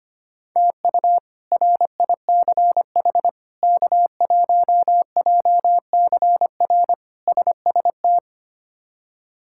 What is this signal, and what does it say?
Morse code, heard as a keyed tone, about 25 wpm, 700 Hz. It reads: TU RICH K1JCR SST
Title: K1USN SST QSO Practice - 25wpm